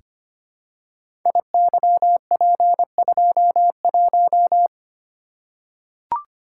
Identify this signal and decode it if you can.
Morse code, heard as a keyed tone, about 25 wpm, 700 Hz. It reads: IYP21
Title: Code Groups - 5 Letter:Numbers - 25wpm